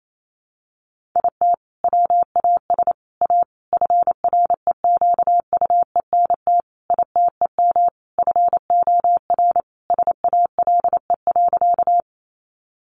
Morse code, heard as a keyed tone, about 28 wpm, 700 Hz. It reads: IT WAS A FREQUENT ITEM FOR SALE.
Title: Sentences from Top 1400 Words 28wpm